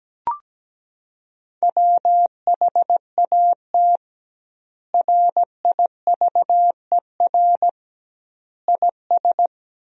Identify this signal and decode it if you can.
Morse code, heard as a keyed tone, about 17 wpm, 700 Hz. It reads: WHAT RIVER IS
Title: 2x Word-Spacing - Sentences from Top 300 Words - 17wpm